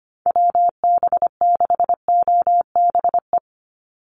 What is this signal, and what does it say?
Morse code, heard as a keyed tone, about 25 wpm, 700 Hz. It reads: WB6OBE
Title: USA Call Signs 25wpm